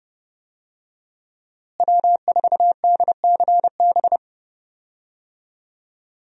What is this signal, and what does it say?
Morse code, heard as a keyed tone, about 30 wpm, 700 Hz. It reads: W4DCB
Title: USA Call Signs - 5 Characters - Repeated 3x - 30wpm